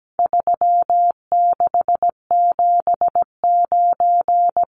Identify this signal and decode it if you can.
Morse code, heard as a keyed tone, about 17 wpm, 700 Hz. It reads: 3679
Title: Numbers - 4-Digit 17wpm